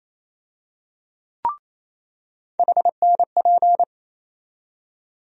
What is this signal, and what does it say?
Morse code, heard as a keyed tone, about 28 wpm, 700 Hz. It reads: HNP
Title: USA Call Sign Postfixes - 3 Letters 28wpm